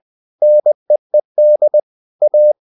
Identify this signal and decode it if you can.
Morse code, heard as a keyed tone, about 20 wpm, 600 Hz. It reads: NEED A
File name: the old man and the sea part 2 20wpm